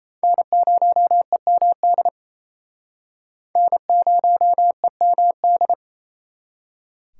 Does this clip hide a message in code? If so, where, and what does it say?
Yes: Morse code from 0.2 to 5.7 s, heard as a keyed tone, about 38 wpm, 700 Hz. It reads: N0EMD N0EMD